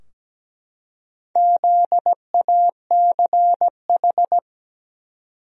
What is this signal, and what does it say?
Morse code, heard as a keyed tone, about 17 wpm, 700 Hz. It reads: ZACH
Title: USA Common Names - Max 4 Letters - 17wpm